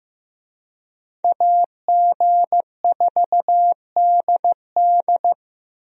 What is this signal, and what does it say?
Morse code, heard as a keyed tone, about 15 wpm, 700 Hz. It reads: AG4DD
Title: USA Call Signs - 5 Characters - Repeated 3x - 15wpm